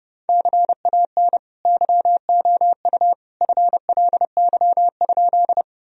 Morse code, heard as a keyed tone, about 30 wpm, 700 Hz. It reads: CAN YOU FLY?
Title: Sentences from Top 1000 Words - Review Part 2 - 30wpm